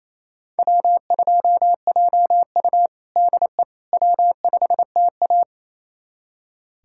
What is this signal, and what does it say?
Morse code, heard as a keyed tone, about 27 wpm, 700 Hz. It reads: W2JU DE W5TA
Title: USA Callsign DE Callsign - 4 Characters - 28wpm